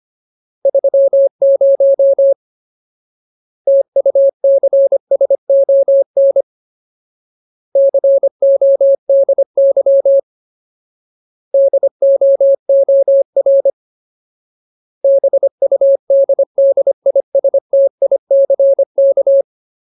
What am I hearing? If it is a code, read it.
Morse code, heard as a keyed tone, about 25 wpm, 550 Hz. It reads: 30 TUCSON CODY DOOR BUDDISTICK
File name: Ultimate Word List / 25WPM Ultimate